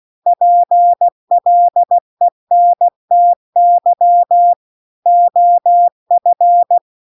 Morse code, heard as a keyed tone, about 16 wpm, 700 Hz. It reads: PLENTY OF